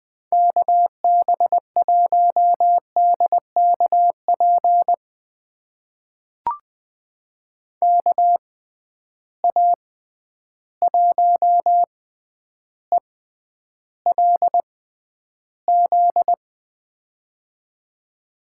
Morse code, heard as a keyed tone, about 20 wpm, spaced as for 10 wpm, 700 Hz. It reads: KB1DKP KA1ELZ
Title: USA Call Signs - 6 Characters - 6x Character Spacing - 20wpm